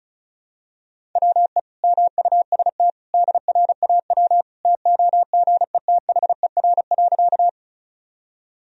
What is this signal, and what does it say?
Morse code, heard as a keyed tone, about 34 wpm, 700 Hz. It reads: WE MUST DRAW TOGETHER.
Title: Sentences from Top 1000 Words - Review Part 1 - 35wpm